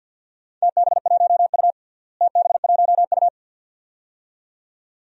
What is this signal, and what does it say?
Morse code, heard as a keyed tone, about 49 wpm, 700 Hz. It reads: TB1U TB1U
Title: DXCC-100 Call Signs - 4 Characters - Repeated 2x - 50wpm